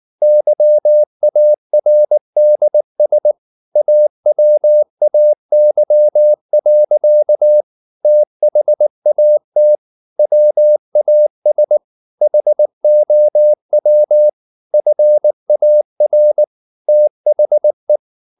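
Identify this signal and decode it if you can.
Morse code, heard as a keyed tone, about 19 wpm, 600 Hz. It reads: YARDS AWAY. THAT WAS HOW FAR THE
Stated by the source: Morse code at 19 wpm from CW med Gnister, reading Ken Follett's novel Never in English.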